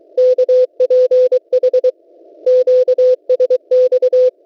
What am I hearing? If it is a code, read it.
Morse code, heard as a keyed tone, about 23 wpm, 510 Hz. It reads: KPH QSX
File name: WebSDR records / w7pua-2.ddns.net 12808.50 cw